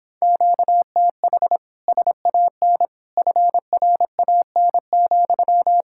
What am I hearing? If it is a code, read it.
Morse code, heard as a keyed tone, about 26 wpm, 700 Hz. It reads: QTH SAN FRAN,